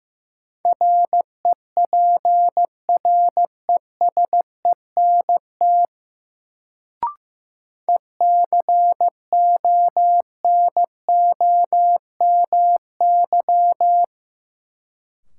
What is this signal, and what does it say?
Morse code, heard as a keyed tone, about 15 wpm, 700 Hz. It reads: REPRESENT ECONOMY
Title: Sentences from Top 1200 Words - New Word Drill Alt - 15wpm